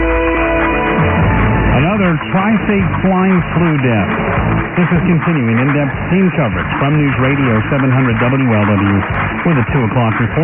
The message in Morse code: none

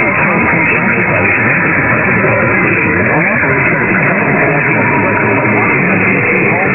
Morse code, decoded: none

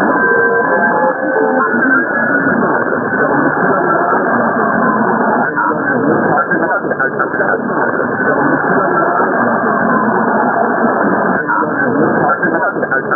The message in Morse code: none

